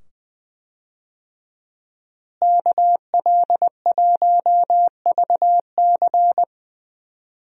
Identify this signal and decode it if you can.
Morse code, heard as a keyed tone, about 20 wpm, 700 Hz. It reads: KL1VC
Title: USA Call Signs - 5 Characters - Repeated 2x - 20wpm